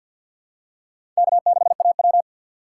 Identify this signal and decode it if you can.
Morse code, heard as a keyed tone, about 52 wpm, 700 Hz. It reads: K6AW
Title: USA Call Signs - 4 Characters - 6x Character Spacing - 50wpm